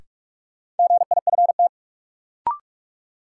Morse code, heard as a keyed tone, about 44 wpm, 700 Hz. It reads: GIFT
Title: Sentences from Top 900 Words - New Word Drill - 45wpm